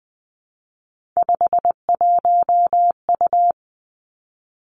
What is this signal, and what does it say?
Morse code, heard as a keyed tone, about 20 wpm, 700 Hz. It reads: 51U
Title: Three Challenging Letter-Number Combinations 20wpm